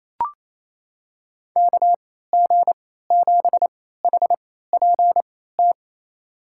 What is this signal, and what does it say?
Morse code, heard as a keyed tone, about 28 wpm, spaced as for 16 wpm, 700 Hz. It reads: KG7HPT
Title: USA Call Signs - 3x Character Spacing - 28wpm